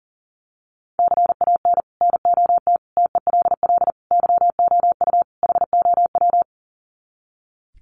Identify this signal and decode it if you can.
Morse code, heard as a keyed tone, about 40 wpm, 700 Hz. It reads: CAN NOT TELL YOU HOW
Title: Sets of 5 Words Encore2 40wpm